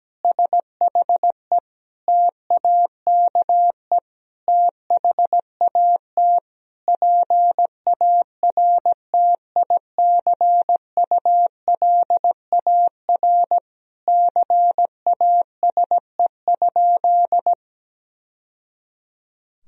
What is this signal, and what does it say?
Morse code, heard as a keyed tone, about 17 wpm, 700 Hz. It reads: SHE TAKE THAT PARTICULAR CASE?
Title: Sentences from Top 1000 Words - Review Part 4 - 17wpm